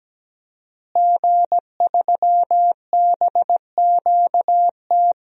Morse code, heard as a keyed tone, about 17 wpm, 700 Hz. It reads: G3BQT
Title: DXCC-100 Call Signs - 5 Characters - Repeated 3x - 17wpm